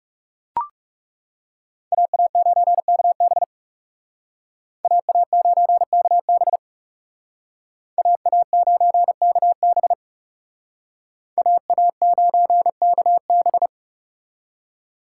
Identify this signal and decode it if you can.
Morse code, heard as a keyed tone, about 37 wpm, 700 Hz. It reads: AA9KB AA9KB AA9KB AA9KB
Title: Speed Racing - USA Call Signs 30wpm